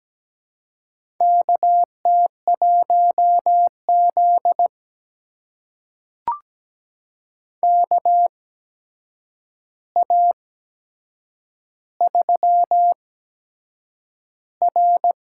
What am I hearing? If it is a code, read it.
Morse code, heard as a keyed tone, about 17 wpm, spaced as for 7 wpm, 700 Hz. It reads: KT1Z KA3R
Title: USA Call Signs - 4 Characters - 8x Character Spacing - 17wpm